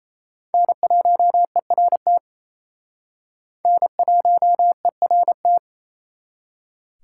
Morse code, heard as a keyed tone, about 34 wpm, 700 Hz. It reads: N1ERT N1ERT
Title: Speed Racing - USA Call Signs 28wpm